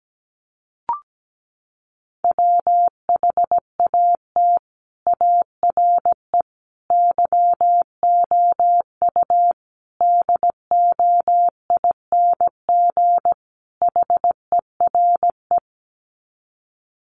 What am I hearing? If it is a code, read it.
Morse code, heard as a keyed tone, about 17 wpm, 700 Hz. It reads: WHAT ARE YOU DOING HERE
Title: Sets of 5 Words 17wpm